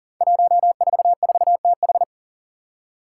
Morse code, heard as a keyed tone, about 40 wpm, 700 Hz. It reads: 144TH
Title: Five Challenging Letter-Number Combinations 40wpm